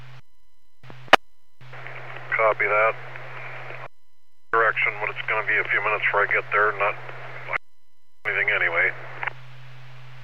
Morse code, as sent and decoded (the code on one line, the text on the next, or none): none